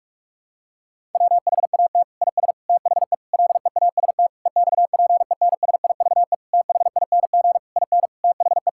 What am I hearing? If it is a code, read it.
Morse code, heard as a keyed tone, about 46 wpm, 700 Hz. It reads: WHAT IS THE LEAST EXPENSIVE THING IN THE
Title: Sentences from Top 1000 Words - Review Part 6 - 45wpm